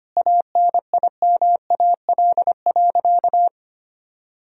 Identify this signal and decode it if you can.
Morse code, heard as a keyed tone, about 25 wpm, 700 Hz. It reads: ANIMAL.
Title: Sentences from Top 1000 Words - Review Part 1 - 25wpm